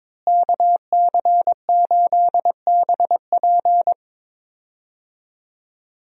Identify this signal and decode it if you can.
Morse code, heard as a keyed tone, about 22 wpm, 700 Hz. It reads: KC8BP
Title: USA Call Signs - 5 Characters - Repeated 2x - 22wpm